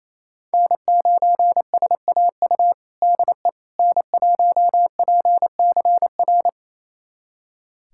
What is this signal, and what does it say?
Morse code, heard as a keyed tone, about 28 wpm, 700 Hz. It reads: N9SAU DE N1PCR
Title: USA Callsign DE Callsign - 5 Characters - 28wpm